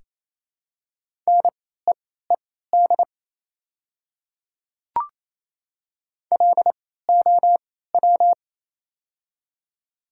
Morse code, heard as a keyed tone, about 28 wpm, spaced as for 16 wpm, 700 Hz. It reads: NEED LOW
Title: 3x Character Spacing - Top 200 Words - 28wpm